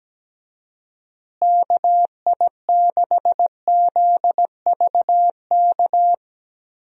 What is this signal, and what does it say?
Morse code, heard as a keyed tone, about 17 wpm, 700 Hz. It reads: KI6ZVK
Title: USA Call Signs - 6 Characters - Repeated 3x - 17wpm